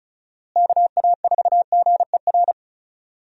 Morse code, heard as a keyed tone, about 34 wpm, 700 Hz. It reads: KA4GER
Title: USA Call Signs - 6 Characters - Repeated 2x - 35wpm